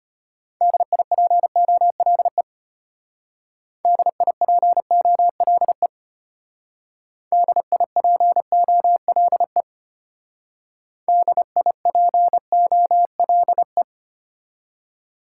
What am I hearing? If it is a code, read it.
Morse code, heard as a keyed tone, about 32 wpm, 700 Hz. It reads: DIPOLE DIPOLE DIPOLE DIPOLE